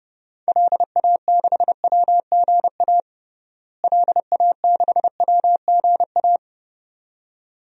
Morse code, heard as a keyed tone, about 29 wpm, 700 Hz. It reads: LA6WGA LA6WGA
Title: DXCC-100 Call Signs - 6 Characters - Repeated 2x - 30wpm